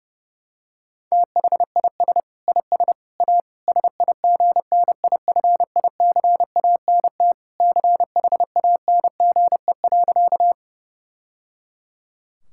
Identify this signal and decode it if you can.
Morse code, heard as a keyed tone, about 32 wpm, 700 Hz. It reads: THIS IS A SIGNIFICANT CHANGE.
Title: Sentences from Top 1000 Words - Review Part 6 - 30wpm